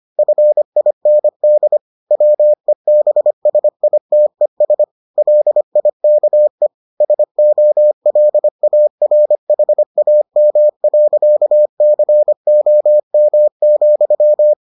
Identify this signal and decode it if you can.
Morse code, heard as a keyed tone, about 25 wpm, 600 Hz. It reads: FIND WEBSITES LIKE SOLARHAM.COM,